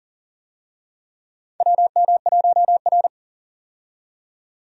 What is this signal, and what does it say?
Morse code, heard as a keyed tone, about 40 wpm, 690 Hz. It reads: WM1R
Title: USA Call Signs - 4 Characters - 40wpm